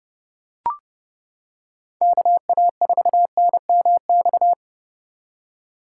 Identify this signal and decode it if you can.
Morse code, heard as a keyed tone, about 30 wpm, 700 Hz. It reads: KA4NMX